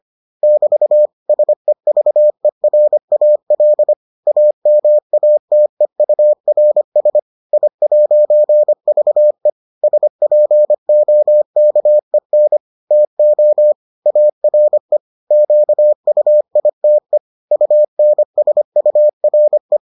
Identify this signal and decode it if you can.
Morse code, heard as a keyed tone, about 25 wpm, 600 Hz. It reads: <BT> SEVERAL AMATEURS I'VE SPOKEN TO ARE QUITE UNSURE